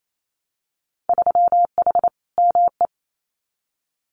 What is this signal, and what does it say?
Morse code, heard as a keyed tone, about 28 wpm, 700 Hz. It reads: 3H ME
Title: Winter Field Day Exchanges Encore 28wpm